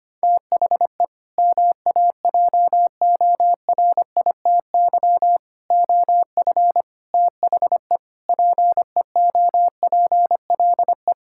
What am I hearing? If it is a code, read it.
Morse code, heard as a keyed tone, about 25 wpm, 700 Hz. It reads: THE MAJORITY OF THE PEOPLE
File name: practice / 25wpm / Sentences with 4-Word Phrases - 25wpm